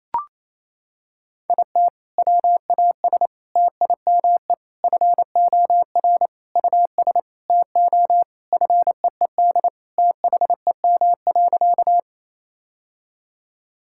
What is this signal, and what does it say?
Morse code, heard as a keyed tone, about 28 wpm, 700 Hz. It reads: IT WAS TIME FOR US TO FEED THEM.
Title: Sentences from Top 1600 Words 28wpm